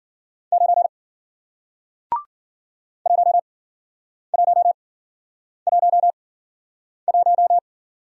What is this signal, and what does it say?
Morse code, heard as a keyed tone, about 53 wpm, spaced as for 9 wpm, 700 Hz. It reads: 9 1111